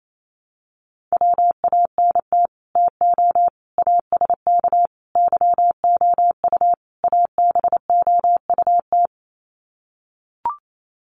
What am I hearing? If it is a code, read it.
Morse code, heard as a keyed tone, about 28 wpm, 700 Hz. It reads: WANT TO ASK YOU ABOUT
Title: Sets of 5 Words Encore2 28wpm